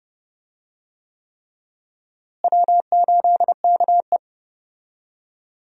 Morse code, heard as a keyed tone, about 30 wpm, 700 Hz. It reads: W8KE
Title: USA Call Signs - 4 Characters - Repeated 3x - 30wpm